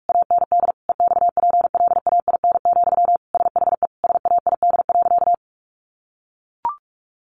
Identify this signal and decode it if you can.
Morse code, heard as a keyed tone, about 44 wpm, 700 Hz. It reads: AND EXPLAIN, SHE SAID.
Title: Sentences from Top 1100 Words 45wpm